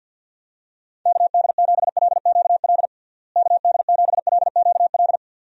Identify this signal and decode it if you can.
Morse code, heard as a keyed tone, about 50 wpm, 700 Hz. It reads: KD7LQL KD7LQL
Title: USA Call Signs - 6 Characters - Repeated 2x - 50wpm